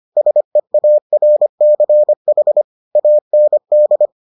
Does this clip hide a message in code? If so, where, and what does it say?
Yes: Morse code from 0.2 to 4.1 s, heard as a keyed tone, about 25 wpm, 600 Hz. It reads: SEARCH AND